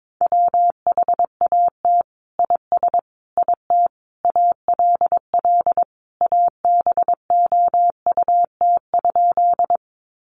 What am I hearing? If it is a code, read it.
Morse code, heard as a keyed tone, about 22 wpm, 700 Hz. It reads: WHAT IS IT ALL ABOUT?
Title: Sentences from Top 100 Words 22wpm